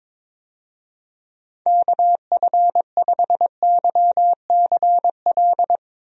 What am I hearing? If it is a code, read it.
Morse code, heard as a keyed tone, about 22 wpm, 700 Hz. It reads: KF5YCL
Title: USA Call Signs - 6 Characters - Repeated 3x - 22wpm